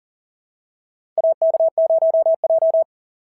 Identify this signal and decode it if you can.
Morse code, heard as a keyed tone, about 40 wpm, 660 Hz. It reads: AK0J
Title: SOTA QSO Practice 40wpm